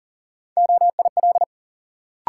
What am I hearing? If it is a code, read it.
Morse code, heard as a keyed tone, about 40 wpm, 700 Hz. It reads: OIL
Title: Sentences from Top 1000 Words - Review Part 4 - New Word Drill - 40wpm